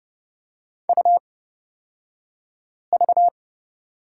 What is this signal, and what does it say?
Morse code, heard as a keyed tone, about 30 wpm, spaced as for 5 wpm, 690 Hz. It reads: UV
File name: Lesson 36d - BT - Two Characters 30wpm